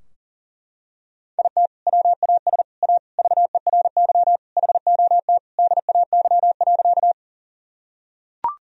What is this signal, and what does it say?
Morse code, heard as a keyed tone, about 37 wpm, 700 Hz. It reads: IT WAS A VERY HOT DAY.
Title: Sentences from Top 1000 Words - Review Part 3 - 40wpm